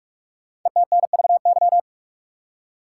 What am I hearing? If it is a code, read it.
Morse code, heard as a keyed tone, about 44 wpm, 700 Hz. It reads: ETNVY
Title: Code Groups - 5 Letters - 45wpm